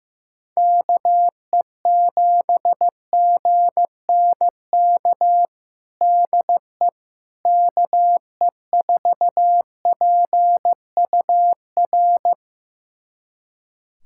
Morse code, heard as a keyed tone, about 15 wpm, 700 Hz. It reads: KE7GNK DE KE4PUR
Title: USA Callsign DE Callsign - 6 Characters - 15wpm